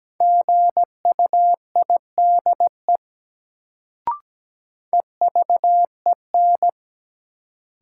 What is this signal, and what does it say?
Morse code, heard as a keyed tone, about 17 wpm, 700 Hz. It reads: GUIDE EVEN
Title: Sentences from Top 600 Words - New Word Drill - 17wpm